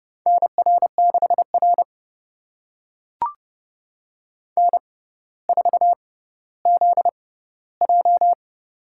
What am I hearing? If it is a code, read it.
Morse code, heard as a keyed tone, about 30 wpm, spaced as for 15 wpm, 690 Hz. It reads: NR6R N4ZJ
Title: USA Call Signs - 4 Characters - 6x Character Spacing - 30wpm